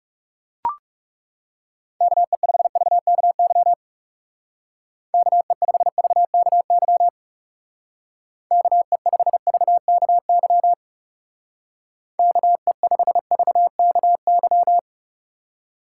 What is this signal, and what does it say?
Morse code, heard as a keyed tone, about 36 wpm, 700 Hz. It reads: KE5VKY KE5VKY KE5VKY KE5VKY
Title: Speed Racing - USA 2x3 Call Signs 30wpm